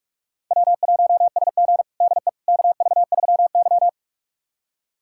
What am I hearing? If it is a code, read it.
Morse code, heard as a keyed tone, about 46 wpm, 700 Hz. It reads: W1SG DE KV3Y
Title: USA Callsign DE Callsign - 4 Characters - 45wpm